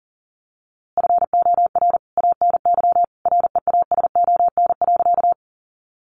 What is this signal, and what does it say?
Morse code, heard as a keyed tone, about 40 wpm, 700 Hz. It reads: FOR ANY REASON.
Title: A Sound Of Thunder 40wpm